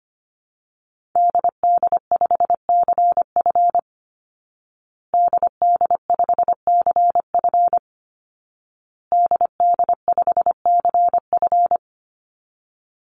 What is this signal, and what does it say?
Morse code, heard as a keyed tone, about 25 wpm, 700 Hz. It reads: DD5CF DD5CF DD5CF
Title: Call Signs Repeated 2x - 25wpm